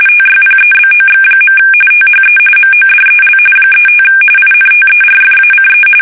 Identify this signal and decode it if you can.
RTTY, about 45.45 baud. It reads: TYPEBOX. THE MODEL 35 INTERFACE
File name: fsk850 / teletype-corp